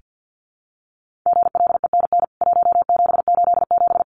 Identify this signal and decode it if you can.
Morse code, heard as a keyed tone, about 48 wpm, 700 Hz. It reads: GLENN 1787
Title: CWT Exchange 50wpm